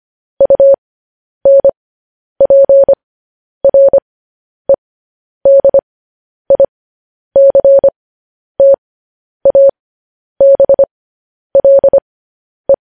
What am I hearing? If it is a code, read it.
Morse code, heard as a keyed tone, about 25 wpm, spaced as for 10 wpm, 550 Hz. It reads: UNPREDICTABLE
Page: news-headlines-in-morse-code-at-10-wpm